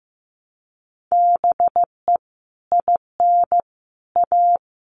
Morse code, heard as a keyed tone, about 15 wpm, 700 Hz. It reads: BE IN A
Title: Sets of 5 Words Encore2 15wpm